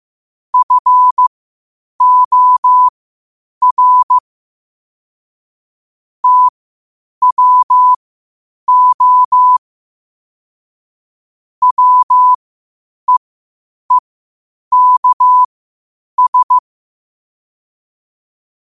Morse code, heard as a keyed tone, about 15 wpm, spaced as for 8 wpm, 1000 Hz. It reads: FOR TWO WEEKS
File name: qotdfiles / 8wpm